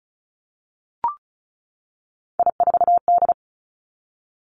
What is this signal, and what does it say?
Morse code, heard as a keyed tone, about 36 wpm, 700 Hz. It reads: I4D
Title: Three Challenging Letter-Number Combinations 35wpm